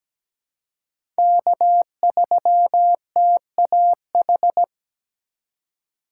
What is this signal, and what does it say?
Morse code, heard as a keyed tone, about 17 wpm, 700 Hz. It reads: K3TAH
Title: USA Call Signs - 5 Characters - 2x Character Spacing - 17wpm